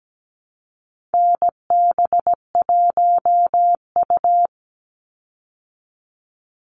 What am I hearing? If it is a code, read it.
Morse code, heard as a keyed tone, about 17 wpm, 700 Hz. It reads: NB1U
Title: Call Signs Repeated 2x - 17wpm